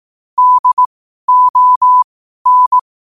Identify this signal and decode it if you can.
Morse code, heard as a keyed tone, about 18 wpm, spaced as for 12 wpm, 1000 Hz. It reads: DON
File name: qotdfiles / 12wpm